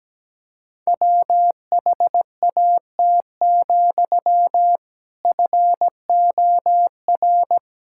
Morse code, heard as a keyed tone, about 17 wpm, 700 Hz. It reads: WHAT, FOR
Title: Sentences from Top 1000 Words - Review Part 5 - 17wpm